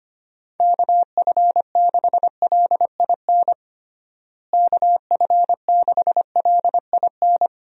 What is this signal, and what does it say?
Morse code, heard as a keyed tone, about 25 wpm, 700 Hz. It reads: KF6LIN KF6LIN
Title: USA Call Signs - 6 Characters - Repeated 2x - 25wpm